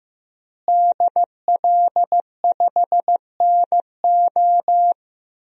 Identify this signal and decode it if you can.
Morse code, heard as a keyed tone, about 15 wpm, 700 Hz. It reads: DL5NO
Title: DXCC-100 Call Signs - 5 Characters - Repeated 3x - 15wpm